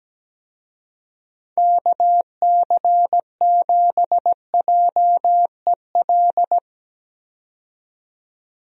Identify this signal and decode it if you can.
Morse code, heard as a keyed tone, about 17 wpm, 700 Hz. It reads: KC7JEL
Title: USA Call Signs - 6 Characters - 17wpm